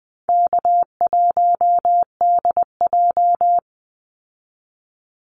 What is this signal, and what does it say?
Morse code, heard as a keyed tone, about 19 wpm, 700 Hz. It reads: K1DJ
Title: Easy Call Signs Repeated 2x - 20wpm